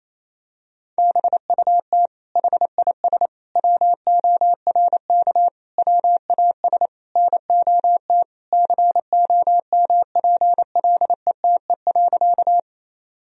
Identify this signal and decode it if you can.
Morse code, heard as a keyed tone, about 28 wpm, 700 Hz. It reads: BUT HIS WORK WAS NOT COMPLETE.
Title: Sentences from Top 1000 Words - Review Part 2 - 28wpm